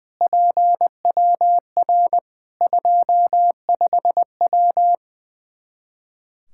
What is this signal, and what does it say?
Morse code, heard as a keyed tone, about 21 wpm, 700 Hz. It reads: PWR 25W
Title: Practice QSO - Exchange 2 - Info - 20wpm